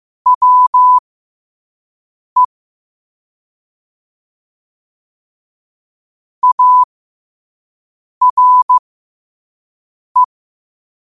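Morse code, heard as a keyed tone, about 15 wpm, spaced as for 5 wpm, 1000 Hz. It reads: WE ARE
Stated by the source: 5 wpm Morse code practice recording.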